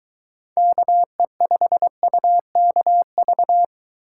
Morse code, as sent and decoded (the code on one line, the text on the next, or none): -.- . ..... ..- -.- ...-
KE5UKV